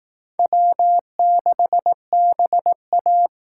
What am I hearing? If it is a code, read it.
Morse code, heard as a keyed tone, about 19 wpm, 700 Hz. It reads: W6BA
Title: Speed Racing - USA Call Signs 15wpm